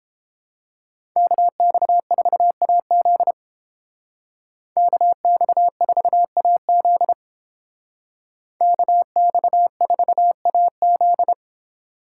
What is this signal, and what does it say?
Morse code, heard as a keyed tone, about 29 wpm, 700 Hz. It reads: KX4AZ KX4AZ KX4AZ